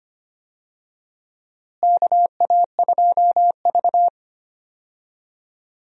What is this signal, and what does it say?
Morse code, heard as a keyed tone, about 25 wpm, 700 Hz. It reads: KA2V
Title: USA Call Signs - 4 Characters - Repeated 3x - 25wpm